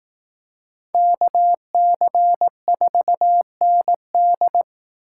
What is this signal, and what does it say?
Morse code, heard as a keyed tone, about 18 wpm, 700 Hz. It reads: KC4ND
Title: Speed Racing - USA 2x2 Call Signs 15wpm